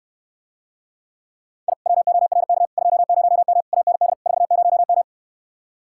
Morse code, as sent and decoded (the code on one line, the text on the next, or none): .. / .-- --- .-. -.- / ..--- ----- -- / .- -. -.. / ....- ----- --
I WORK 20M AND 40M